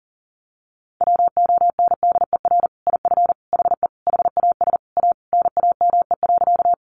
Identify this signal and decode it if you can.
Morse code, heard as a keyed tone, about 40 wpm, 700 Hz. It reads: WONDER IF HE HAS A NAME.
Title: Sentences from Top 500 Words 40wpm